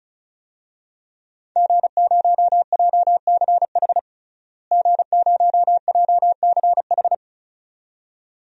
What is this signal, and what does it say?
Morse code, heard as a keyed tone, about 34 wpm, 700 Hz. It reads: G0JCH G0JCH
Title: DXCC-100 Call Signs - 5 Characters - Repeated 2x - 35wpm